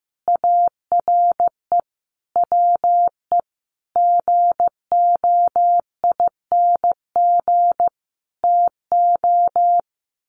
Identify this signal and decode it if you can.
Morse code, heard as a keyed tone, about 15 wpm, 700 Hz. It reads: ARE WE GOING TO
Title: Sets of 5 Words 15wpm